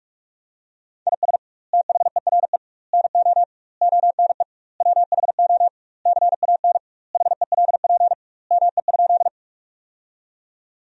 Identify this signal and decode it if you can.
Morse code, heard as a keyed tone, about 41 wpm, 700 Hz. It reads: IS THERE NO ONE WHO CAN HELP ME?
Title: 2x Word-Spacing - Sentences from Top 200 Words - 45wpm